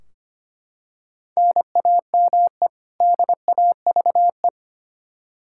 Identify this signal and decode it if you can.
Morse code, heard as a keyed tone, about 25 wpm, 700 Hz. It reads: NAME DAVE
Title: Practice QSO - Exchange 1 - Beginning - 25wpm